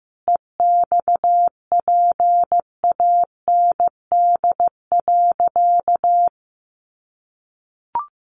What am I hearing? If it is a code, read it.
Morse code, heard as a keyed tone, about 15 wpm, 700 Hz. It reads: EXPAND.
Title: Sentences from Top 1400 Words 15wpm